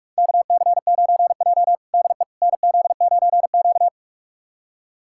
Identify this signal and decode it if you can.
Morse code, heard as a keyed tone, about 46 wpm, 690 Hz. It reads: KC9J DE NZ9Q
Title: USA Callsign DE Callsign - 4 Characters - 45wpm